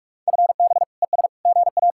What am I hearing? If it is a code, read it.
Morse code, heard as a keyed tone, about 59 wpm, spaced as for 52 wpm, 700 Hz. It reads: FB ES GA
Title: Speed-Racing Common QSO Phrases - 45wpm